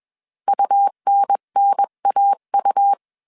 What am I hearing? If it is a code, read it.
Morse code, heard as a keyed tone, about 21 wpm, 800 Hz. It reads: UNNAU
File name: 20wpm